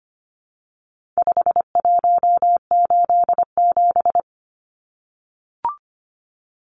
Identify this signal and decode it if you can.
Morse code, heard as a keyed tone, about 25 wpm, 700 Hz. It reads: 5187
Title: Numbers - 4-Digit 25wpm